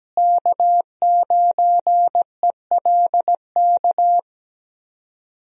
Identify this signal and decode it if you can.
Morse code, heard as a keyed tone, about 17 wpm, 700 Hz. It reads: K9ELK
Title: USA Call Signs - 5 Characters - Repeated 2x - 17wpm